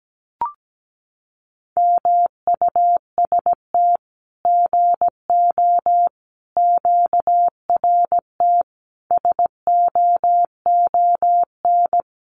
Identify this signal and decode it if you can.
Morse code, heard as a keyed tone, about 17 wpm, 700 Hz. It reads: MUST GO QRT SOON
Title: Common QSO Phrases 17wpm